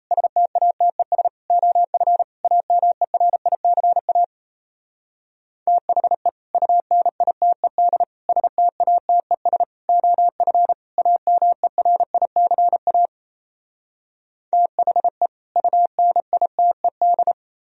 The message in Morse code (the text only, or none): STATES OF AMERICA THE UNITED STATES OF AMERICA THE UNITED